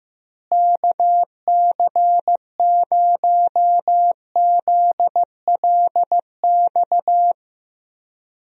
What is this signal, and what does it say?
Morse code, heard as a keyed tone, about 15 wpm, 700 Hz. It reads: KC0ZLX
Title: USA Call Signs - 6 Characters - Repeated 2x - 15wpm